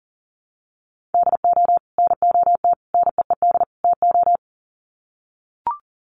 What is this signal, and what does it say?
Morse code, heard as a keyed tone, about 40 wpm, 700 Hz. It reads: DO NOT NEED TO
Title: Sets of 4 Words 40wpm